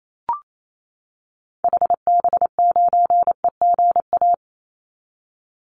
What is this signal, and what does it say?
Morse code, heard as a keyed tone, about 28 wpm, 700 Hz. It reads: HB9EGA
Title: Call Signs - 28wpm